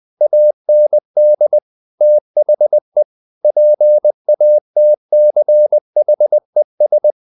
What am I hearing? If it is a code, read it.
Morse code, heard as a keyed tone, about 20 wpm, 600 Hz. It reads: AND THE PATCHES
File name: the old man and the sea / the old man and the sea part 1 20wpm